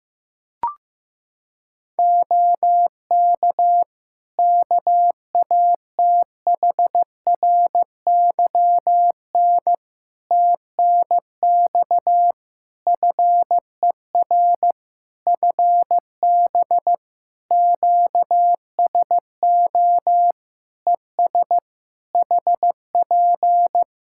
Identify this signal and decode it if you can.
Morse code, heard as a keyed tone, about 15 wpm, 700 Hz. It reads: OK KATHRYN TNX FER FB QSO ES HP
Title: Practice QSO - Exchange 4 - Ending - 15wpm